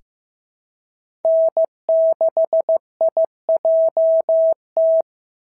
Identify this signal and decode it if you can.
Morse code, heard as a keyed tone, about 15 wpm, 650 Hz. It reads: N6IJT
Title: SOTA QSO Practice 15wpm